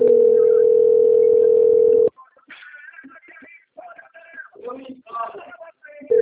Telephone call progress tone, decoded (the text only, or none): ringback tone